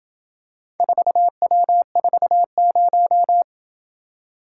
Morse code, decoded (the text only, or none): 4W40